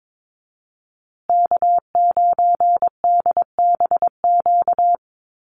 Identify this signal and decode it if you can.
Morse code, heard as a keyed tone, about 22 wpm, 700 Hz. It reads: K9DBQ
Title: USA Call Signs Repeated 3x 22wpm